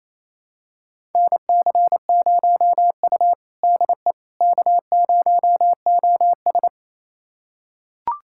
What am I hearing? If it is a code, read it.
Morse code, heard as a keyed tone, about 28 wpm, 700 Hz. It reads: NC0U DE K0OS
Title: USA Callsign DE Callsign - 4 Characters - 28wpm